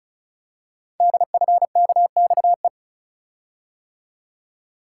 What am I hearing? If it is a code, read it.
Morse code, heard as a keyed tone, about 36 wpm, 690 Hz. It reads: DFKXE
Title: Code Groups - 5 Letters - 35wpm